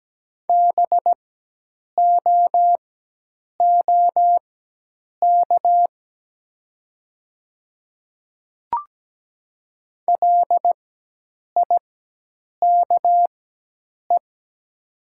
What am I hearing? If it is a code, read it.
Morse code, heard as a keyed tone, about 17 wpm, spaced as for 8 wpm, 700 Hz. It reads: BOOK LIKE
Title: 4x Character Spacing - Top 300 Words - 17wpm